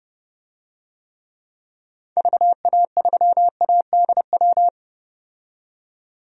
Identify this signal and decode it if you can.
Morse code, heard as a keyed tone, about 32 wpm, 700 Hz. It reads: VA3ADW
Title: DXCC-100 Call Signs - 6 Characters - 30wpm